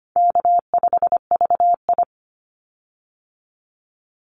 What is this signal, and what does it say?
Morse code, heard as a keyed tone, about 25 wpm, 700 Hz. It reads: K5VI
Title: Easy Call Signs Repeated 2x - 25wpm